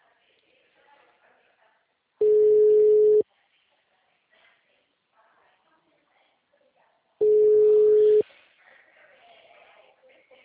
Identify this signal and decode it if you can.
Ringback tone (Europe, 425 Hz), with speech.
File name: out- -2006- - - .15